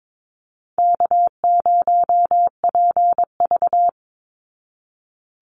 Morse code, heard as a keyed tone, about 22 wpm, 700 Hz. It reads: K0PV
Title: Easy Call Signs Repeated 3x - 22wpm